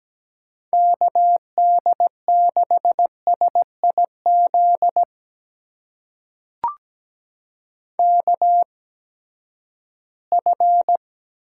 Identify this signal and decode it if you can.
Morse code, heard as a keyed tone, about 17 wpm, spaced as for 12 wpm, 700 Hz. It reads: KD6SIZ KF
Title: USA Call Signs - 6 Characters - 8x Character Spacing - 17wpm